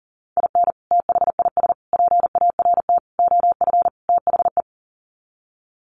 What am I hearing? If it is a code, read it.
Morse code, heard as a keyed tone, about 40 wpm, 700 Hz. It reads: IN THIS PART OF THE
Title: Sets of 5 Words 40wpm